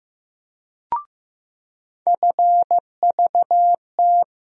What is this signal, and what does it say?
Morse code, heard as a keyed tone, about 15 wpm, 700 Hz. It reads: FVT
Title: USA Call Sign Postfixes - 3 Letters 15wpm